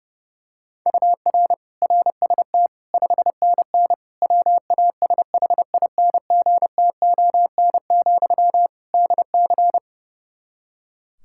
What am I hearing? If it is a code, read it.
Morse code, heard as a keyed tone, about 32 wpm, 700 Hz. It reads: UR RST 5NN WASHINGTON, DC
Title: Practice QSO - Exchange 1 - Beginning - 30wpm